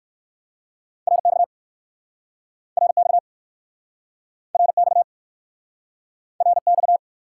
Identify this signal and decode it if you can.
Morse code, heard as a keyed tone, about 59 wpm, 700 Hz. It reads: RX RX RX RX